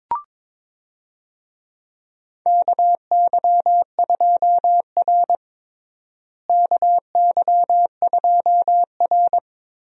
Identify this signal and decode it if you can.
Morse code, heard as a keyed tone, about 22 wpm, 700 Hz. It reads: KY2R KY2R
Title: USA Call Signs - 4 Characters - Repeated 3x - 22wpm